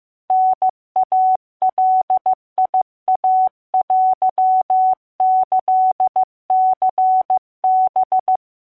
Morse code (Japanese Wabun code), heard as a keyed tone, about 15 wpm, 750 Hz. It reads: タイガイテキニハ